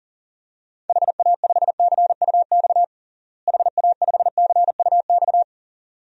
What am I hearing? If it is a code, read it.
Morse code, heard as a keyed tone, about 40 wpm, 700 Hz. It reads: HA5CUX HA5CUX
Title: DXCC-100 Call Signs - 6 Characters - Repeated 2x - 40wpm